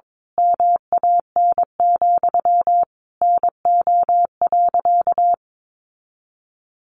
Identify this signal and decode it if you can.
Morse code, heard as a keyed tone, about 22 wpm, 700 Hz. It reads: MAN, NO.
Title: A Sound Of Thunder 22wpm